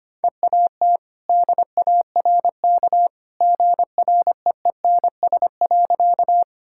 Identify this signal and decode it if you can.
Morse code, heard as a keyed tone, about 25 wpm, 700 Hz. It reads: EAT DARK GREENS.
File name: Easy Sentences with Hints - 25wpm